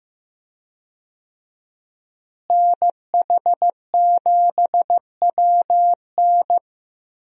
Morse code, heard as a keyed tone, about 15 wpm, 700 Hz. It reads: NH7WN
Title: USA Call Signs - 5 Characters - Repeated 2x - 15wpm